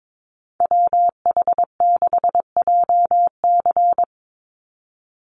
Morse code, heard as a keyed tone, about 22 wpm, 700 Hz. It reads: WH6JC
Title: USA Call Signs 22wpm